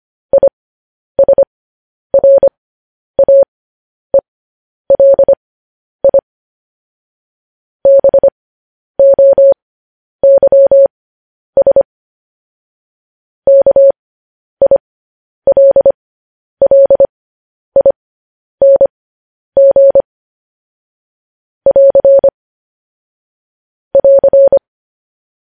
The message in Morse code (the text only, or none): ISRAELI BOYS KILLING <AR> <AR>